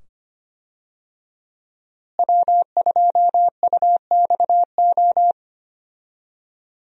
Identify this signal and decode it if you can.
Morse code, heard as a keyed tone, about 25 wpm, 700 Hz. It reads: W2UXO
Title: USA Call Signs - 5 Characters - Repeated 2x - 25wpm